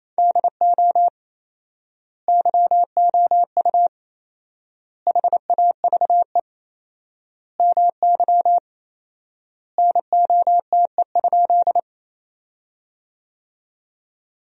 Morse code, heard as a keyed tone, about 28 wpm, 700 Hz. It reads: DO YOU HAVE MY NOTE?
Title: 4x Word-Spacing - Sentences from Top 500 Words - 28wpm